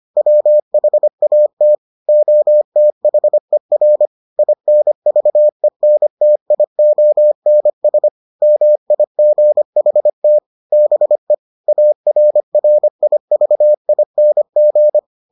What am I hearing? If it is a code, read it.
Morse code, heard as a keyed tone, about 25 wpm, 600 Hz. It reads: WHAT OTHER INVENTIONS MIGHT BE ARRIVING